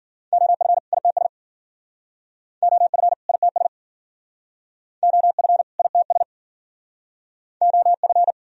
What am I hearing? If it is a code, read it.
Morse code, heard as a keyed tone, about 56 wpm, 700 Hz. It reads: OF ITS OF ITS OF ITS OF